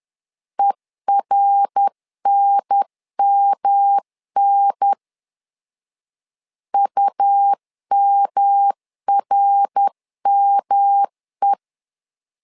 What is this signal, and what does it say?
Morse code, heard as a keyed tone, about 10 wpm, 800 Hz. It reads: ERNMN UMRME
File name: LEZIONI / 3 MWRO / 3 MWR 10wpm 1952